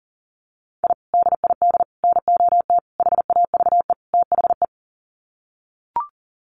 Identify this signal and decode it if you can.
Morse code, heard as a keyed tone, about 40 wpm, 700 Hz. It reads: I DID NOT HAVE THE
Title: Sets of 5 Words Encore 40wpm